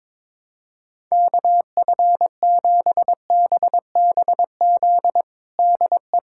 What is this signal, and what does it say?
Morse code, heard as a keyed tone, about 22 wpm, 700 Hz. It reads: KF7BBZ DE
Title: USA Callsign DE Callsign - 6 Characters - 22wpm